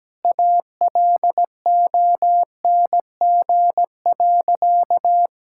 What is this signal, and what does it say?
Morse code, heard as a keyed tone, about 17 wpm, 700 Hz. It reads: ALONG.
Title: Sentences from Top 1600 Words 17wpm